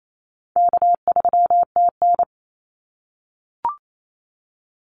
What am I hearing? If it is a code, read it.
Morse code, heard as a keyed tone, about 28 wpm, 700 Hz. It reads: K3TN
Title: Easy Call Signs - 28wpm